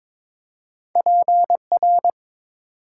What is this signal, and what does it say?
Morse code, heard as a keyed tone, about 22 wpm, 700 Hz. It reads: PR
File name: practice / 22wpm / ARRL Sections v2 - 22wpm